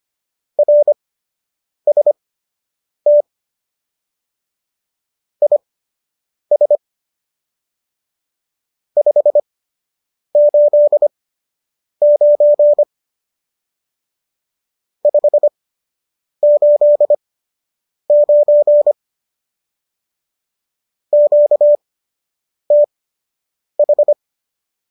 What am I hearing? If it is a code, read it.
Morse code, heard as a keyed tone, about 25 wpm, spaced as for 8 wpm, 590 Hz. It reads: RST IS 589 589 QTH
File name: qso003 08